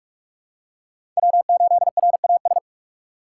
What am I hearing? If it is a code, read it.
Morse code, heard as a keyed tone, about 46 wpm, 700 Hz. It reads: W8RAS
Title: USA Call Signs - 5 Characters - 6x Character Spacing - 45wpm